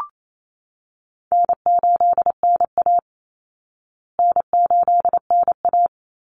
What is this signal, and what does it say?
Morse code, heard as a keyed tone, about 28 wpm, 700 Hz. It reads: N8NA N8NA
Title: Easy Call Signs Repeated 3x - 28wpm